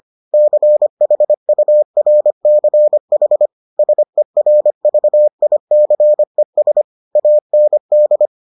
Morse code, heard as a keyed tone, about 25 wpm, 600 Hz. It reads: CHURCH SERVICES AND